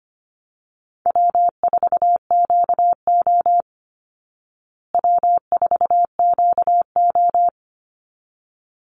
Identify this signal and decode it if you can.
Morse code, heard as a keyed tone, about 25 wpm, 700 Hz. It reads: W4QO W4QO
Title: Easy Call Signs Repeated 2x - 25wpm